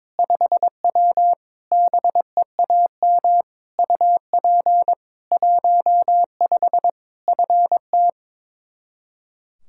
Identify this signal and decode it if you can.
Morse code, heard as a keyed tone, about 22 wpm, 700 Hz. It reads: 5W BEAM UP 15 FT
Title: K1USN SST QSO Practice 22wpm